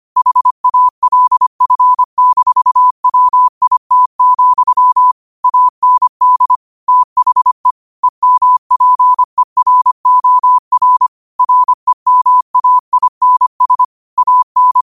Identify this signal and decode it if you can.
Morse code, heard as a keyed tone, about 25 wpm, 1000 Hz. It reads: HALF-WIT, AND THE EMPEROR REMAINS AN
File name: qotdfiles / 25wpm